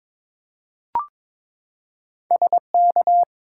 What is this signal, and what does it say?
Morse code, heard as a keyed tone, about 22 wpm, 700 Hz. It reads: SK
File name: ARRL Sections v2 - 22wpm